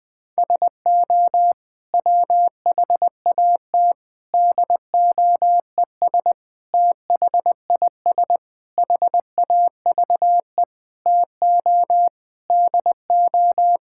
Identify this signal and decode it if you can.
Morse code, heard as a keyed tone, about 19 wpm, 700 Hz. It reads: SO WHAT DOES THIS HAVE TO DO
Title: Sets of 8 Words - 20wpm